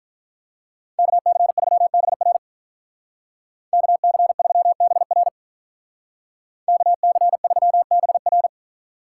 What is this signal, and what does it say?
Morse code, heard as a keyed tone, about 46 wpm, 700 Hz. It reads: KC3BR KC3BR KC3BR